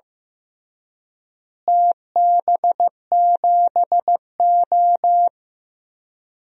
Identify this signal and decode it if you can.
Morse code, heard as a keyed tone, about 15 wpm, 700 Hz. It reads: TB7O
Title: DXCC-100 Call Signs - 4 Characters - Repeated 3x - 15wpm